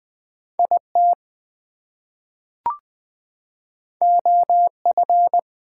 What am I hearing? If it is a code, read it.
Morse code, heard as a keyed tone, about 20 wpm, 700 Hz. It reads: IT OF
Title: Two Letter Words - 20wpm